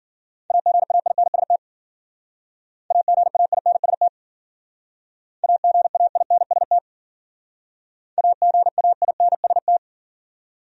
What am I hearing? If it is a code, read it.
Morse code, heard as a keyed tone, about 53 wpm, 700 Hz. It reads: AGAINST AGAINST AGAINST AGAINST